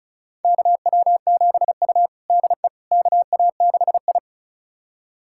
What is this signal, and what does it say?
Morse code, heard as a keyed tone, about 36 wpm, 700 Hz. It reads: KW7U DE KA6I
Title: USA Callsign DE Callsign - 4 Characters - 35wpm